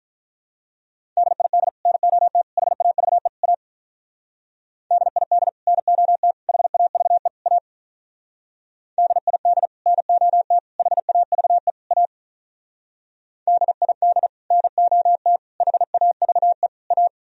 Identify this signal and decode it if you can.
Morse code, heard as a keyed tone, about 44 wpm, 700 Hz. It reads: DID NOT HAVE A DID NOT HAVE A DID NOT HAVE A DID NOT HAVE A
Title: Speed-Racing - Sets of 4 Words 35wpm